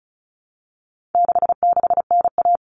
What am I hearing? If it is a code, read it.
Morse code, heard as a keyed tone, about 34 wpm, 700 Hz. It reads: 66NA
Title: All Cut Numbers - 4-Digit 35wpm